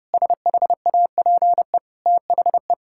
Morse code, heard as a keyed tone, about 30 wpm, 700 Hz. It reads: SHAPE THE
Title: Sentences from Top 1000 Words - Review Part 6 - 30wpm